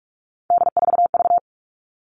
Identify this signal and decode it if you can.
Morse code, heard as a keyed tone, about 44 wpm, 700 Hz. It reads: D4V
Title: All Cut Numbers - 3-Digit 45wpm